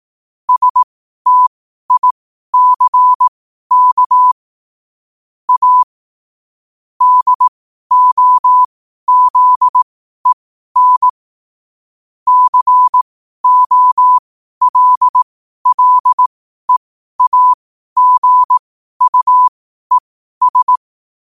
Morse code, heard as a keyed tone, about 18 wpm, spaced as for 12 wpm, 1000 Hz. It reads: STICK A DOZEN COLLEAGUES
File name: qotdfiles / 12wpm